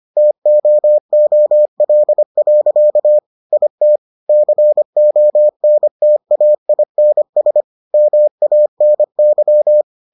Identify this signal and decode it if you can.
Morse code, heard as a keyed tone, about 25 wpm, 600 Hz. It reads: TOOL. IT CONTAINS MANY